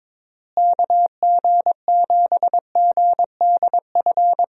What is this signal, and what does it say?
Morse code, heard as a keyed tone, about 21 wpm, 700 Hz. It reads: KG7GDF
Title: USA Call Signs - 6 Characters - Repeated 3x - 22wpm